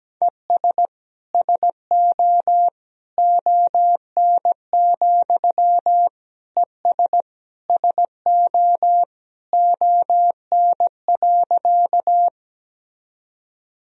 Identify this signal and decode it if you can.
Morse code, heard as a keyed tone, about 17 wpm, 700 Hz. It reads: ES SO ON, ES SO ON.
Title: Sentences from Top 100 Words and Abbreviations 17wpm